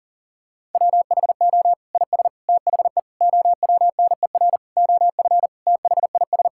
Morse code, heard as a keyed tone, about 37 wpm, 700 Hz. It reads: WHO IS THE OWNER OF THIS
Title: Sentences from Top 1600 Words 40wpm